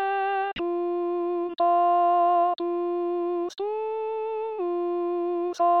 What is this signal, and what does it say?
There is no code.